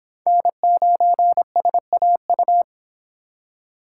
Morse code, heard as a keyed tone, about 26 wpm, 700 Hz. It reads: N9SAU